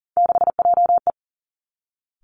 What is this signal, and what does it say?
Morse code, heard as a keyed tone, about 40 wpm, 700 Hz. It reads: 6JE
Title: Three Challenging Letter-Number Combinations 40wpm